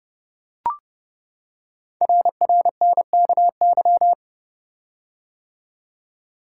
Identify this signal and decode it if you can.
Morse code, heard as a keyed tone, about 30 wpm, 690 Hz. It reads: RRNKY
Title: Code Groups - 5 Letters - 30wpm